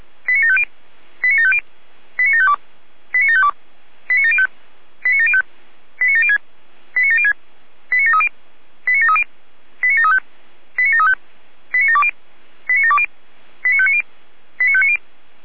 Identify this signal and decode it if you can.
Five-tone selective call (ZVEI), heard as quick runs of five tones, tones of about 70 ms. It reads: 89750 89750 89752 89752 89705 89705 89706 89706 89730 89730 89735 89735 89720 89720 89590 89590